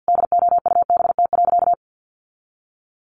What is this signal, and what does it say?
Morse code, heard as a keyed tone, about 50 wpm, 700 Hz. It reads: DOUBT.
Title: Sentences from Top 1500 Words 50wpm